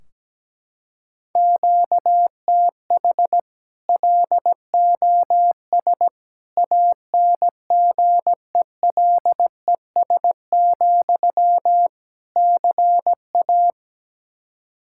Morse code, heard as a keyed tone, about 17 wpm, 700 Hz. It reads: QTH LOS ANGELES, CA
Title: Practice QSO - Exchange 1 - Beginning - 17wpm